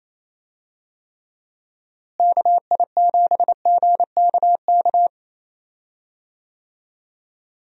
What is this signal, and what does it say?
Morse code, heard as a keyed tone, about 28 wpm, 700 Hz. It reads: KI7GKK
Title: USA Call Signs - 6 Characters - Repeated 3x - 28wpm